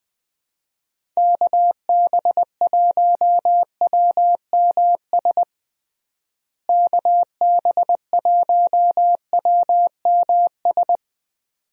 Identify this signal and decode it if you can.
Morse code, heard as a keyed tone, about 21 wpm, 700 Hz. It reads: KB1WMS KB1WMS
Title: USA Call Signs - 6 Characters - Repeated 2x - 20wpm